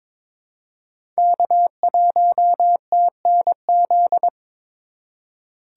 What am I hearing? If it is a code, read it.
Morse code, heard as a keyed tone, about 22 wpm, 700 Hz. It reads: K1TNZ
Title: USA Call Signs - 5 Characters - 3x Character Spacing - 22wpm